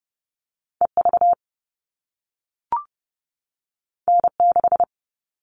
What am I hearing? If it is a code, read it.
Morse code, heard as a keyed tone, about 30 wpm, 700 Hz. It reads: EV N6
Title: All Cut Numbers - 2-Digit 30wpm